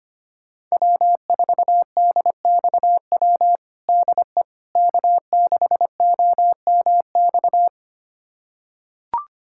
Morse code, heard as a keyed tone, about 25 wpm, 700 Hz. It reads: W4DXW DE K6OMX
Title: USA Callsign DE Callsign - 5 Characters - 25wpm